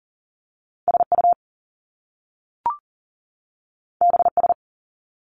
Morse code, heard as a keyed tone, about 41 wpm, 700 Hz. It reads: SU BS
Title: Top Two Letter Combinations in Words 40wpm